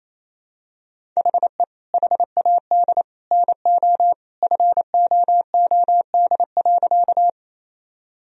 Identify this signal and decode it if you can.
Morse code, heard as a keyed tone, about 28 wpm, 700 Hz. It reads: HE HAD NO FOOD.
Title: Sentences from Top 1000 Words - Review Part 1 - 28wpm